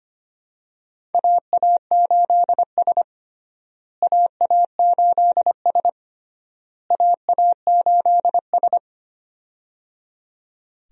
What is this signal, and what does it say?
Morse code, heard as a keyed tone, about 25 wpm, 700 Hz. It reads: AA8S AA8S AA8S
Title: USA Call Signs - 4 Characters - Repeated 3x - 25wpm